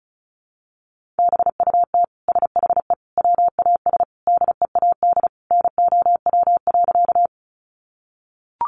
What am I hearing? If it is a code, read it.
Morse code, heard as a keyed tone, about 34 wpm, 700 Hz. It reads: BUT SHE WAS DEAD NOW.
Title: Sentences from Top 1500 Words 35wpm